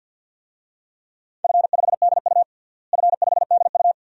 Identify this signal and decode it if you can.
Morse code, heard as a keyed tone, about 52 wpm, 700 Hz. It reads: F5DU F5DU
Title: DXCC-100 Call Signs - 4 Characters - Repeated 2x - 50wpm